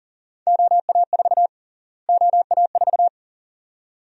Morse code, heard as a keyed tone, about 40 wpm, 700 Hz. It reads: OA4 OA4
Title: USA Call Sign Prefixes - Repeated 2x - 40wpm